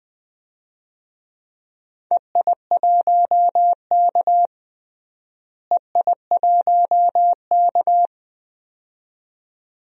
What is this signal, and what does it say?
Morse code, heard as a keyed tone, about 20 wpm, 700 Hz. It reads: EI1K EI1K
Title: DXCC-100 Call Signs - 4 Characters - Repeated 2x - 20wpm